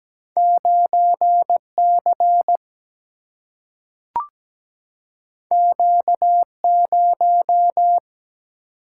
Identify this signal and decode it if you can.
Morse code, heard as a keyed tone, about 17 wpm, 700 Hz. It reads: 9C Q0
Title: Single-Letter Single-Number 17wpm